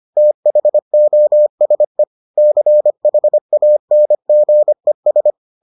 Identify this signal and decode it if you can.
Morse code, heard as a keyed tone, about 25 wpm, 600 Hz. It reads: THOSE CHANGES